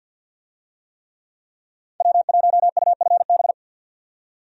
Morse code, heard as a keyed tone, about 50 wpm, 700 Hz. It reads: W1UFB
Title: USA Call Signs - 5 Characters - 50wpm